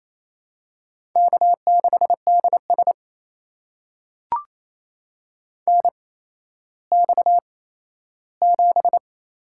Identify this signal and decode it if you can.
Morse code, heard as a keyed tone, about 28 wpm, spaced as for 14 wpm, 700 Hz. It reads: K6DS NX7
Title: USA Call Signs - 4 Characters - 8x Character Spacing - 28wpm